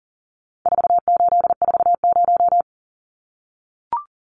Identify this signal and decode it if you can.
Morse code, heard as a keyed tone, about 40 wpm, 700 Hz. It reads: 4840
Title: Numbers - 4-Digit 40wpm